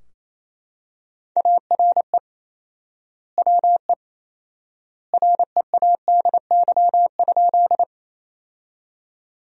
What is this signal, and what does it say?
Morse code, heard as a keyed tone, about 27 wpm, 700 Hz. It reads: ARE WE READY?
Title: 4x Word-Spacing - Sentences from Top 400 Words - 28wpm